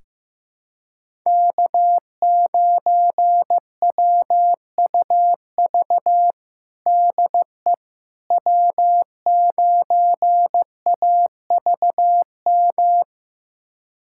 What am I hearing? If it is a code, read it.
Morse code, heard as a keyed tone, about 15 wpm, 700 Hz. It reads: K9WUV DE W9AVM
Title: USA Callsign DE Callsign - 5 Characters - 15wpm